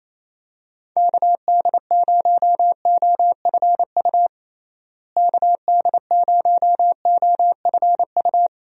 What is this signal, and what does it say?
Morse code, heard as a keyed tone, about 28 wpm, 700 Hz. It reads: KD0OFU KD0OFU
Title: USA Call Signs - 6 Characters - Repeated 2x - 28wpm